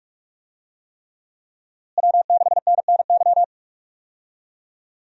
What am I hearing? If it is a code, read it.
Morse code, heard as a keyed tone, about 42 wpm, 700 Hz. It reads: W6NNY
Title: USA Call Signs - 5 Characters - Repeated 3x - 45wpm